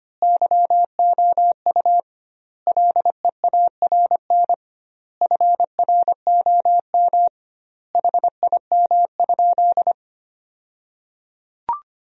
Morse code, heard as a keyed tone, about 25 wpm, 700 Hz. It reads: YOU LEARN FROM HIM?
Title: 2x Word-Spacing - Sentences from Top 300 Words - 25wpm